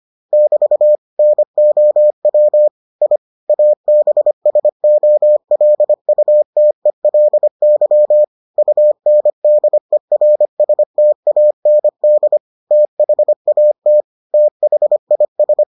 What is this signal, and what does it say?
Morse code, heard as a keyed tone, about 25 wpm, 600 Hz. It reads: <BT> NOW I ABSOLUTELY UNDERSTAND THAT THIS